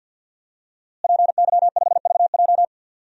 Morse code, heard as a keyed tone, about 47 wpm, 700 Hz. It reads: PY5VJ
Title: DXCC-100 Call Signs - 5 Characters - Repeated 3x - 50wpm